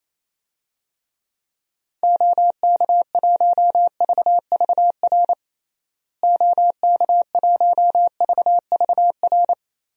Morse code, heard as a keyed tone, about 28 wpm, 700 Hz. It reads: OK1VVR OK1VVR
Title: DXCC-100 Call Signs - 6 Characters - Repeated 3x - 28wpm